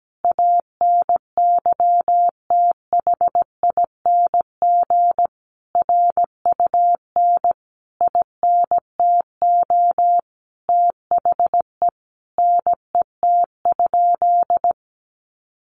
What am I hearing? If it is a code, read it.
Morse code, heard as a keyed tone, about 17 wpm, 700 Hz. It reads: ANYTHING RUN INTO THE NET?
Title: Sentences from Top 1400 Words 17wpm